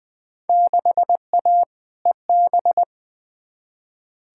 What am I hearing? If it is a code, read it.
Morse code, heard as a keyed tone, about 20 wpm, 700 Hz. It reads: 6A EB
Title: Field Day Exchanges Encore - 20wpm